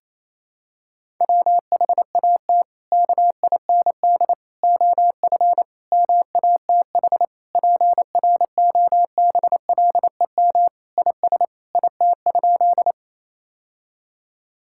Morse code, heard as a keyed tone, about 27 wpm, 700 Hz. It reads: WHAT KIND OF MATH PROBLEM IS IT?
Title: Sentences from Top 1000 Words - Review Part 6 - 28wpm